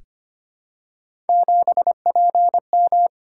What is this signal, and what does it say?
Morse code, heard as a keyed tone, about 25 wpm, 700 Hz. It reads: 7PM
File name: Three Letter-Number Combinations - 25wpm